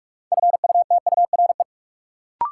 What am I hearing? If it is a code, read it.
Morse code, heard as a keyed tone, about 46 wpm, 700 Hz. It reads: FUTURE
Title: Sets of 3 Words - New Word Drill - 45wpm